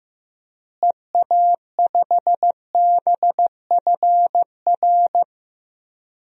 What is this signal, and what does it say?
Morse code, heard as a keyed tone, about 15 wpm, 700 Hz. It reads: EA5BFR
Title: DXCC-100 Call Signs - 6 Characters - Repeated 3x - 15wpm